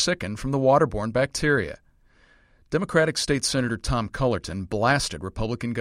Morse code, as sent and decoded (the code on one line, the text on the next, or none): none